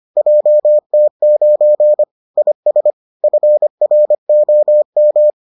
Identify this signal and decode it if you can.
Morse code, heard as a keyed tone, about 25 wpm, 600 Hz. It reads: JT9 IS FROM